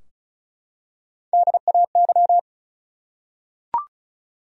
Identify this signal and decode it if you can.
Morse code, heard as a keyed tone, about 36 wpm, 700 Hz. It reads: DAY
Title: Top 100 Words 35wpm